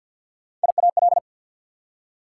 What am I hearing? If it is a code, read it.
Morse code, heard as a keyed tone, about 48 wpm, 690 Hz. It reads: IAL
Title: Prefixes and Suffixes 50wpm